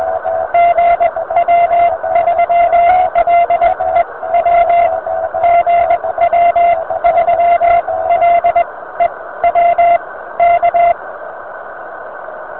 Morse code, heard as a keyed tone, about 20 wpm, spaced as for 16 wpm, 690 Hz, with radio noise. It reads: GW3LEW GW3LEW K
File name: VP9-N0JK-pile